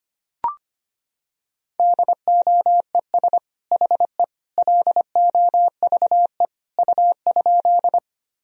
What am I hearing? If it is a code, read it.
Morse code, heard as a keyed tone, about 25 wpm, 700 Hz. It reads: DOES HE LOVE U?
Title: Sentences from Top 400 Words and Abbreviations 25wpm